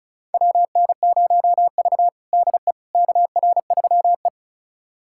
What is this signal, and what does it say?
Morse code, heard as a keyed tone, about 36 wpm, 700 Hz. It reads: WN0V DE KR3E
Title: USA Callsign DE Callsign - 4 Characters - 35wpm